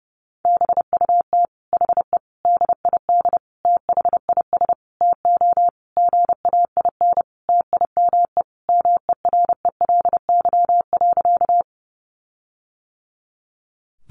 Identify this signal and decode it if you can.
Morse code, heard as a keyed tone, about 30 wpm, 700 Hz. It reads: BUT HE DID THIS TO GAIN TIME MERELY.
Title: Sentences from Top 1400 Words 30wpm